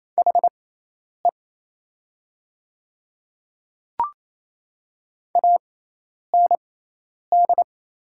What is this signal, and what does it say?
Morse code, heard as a keyed tone, about 27 wpm, spaced as for 10 wpm, 700 Hz. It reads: HE AND
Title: 6x Character Spacing - Top 500 Words - 28wpm